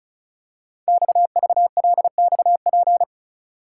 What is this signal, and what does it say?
Morse code, heard as a keyed tone, about 36 wpm, 690 Hz. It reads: XVLXP
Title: Code Groups - 5 Letters - 35wpm